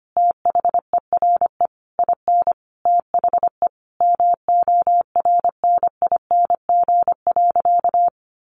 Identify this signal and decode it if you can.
Morse code, heard as a keyed tone, about 25 wpm, 700 Hz. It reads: THERE IN THE MORNING.
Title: Sentences from Top 1200 Words 25wpm